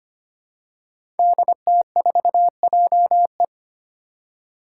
Morse code, heard as a keyed tone, about 25 wpm, 700 Hz. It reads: DT4JE
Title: Five Challenging Letter-Number Combinations 25wpm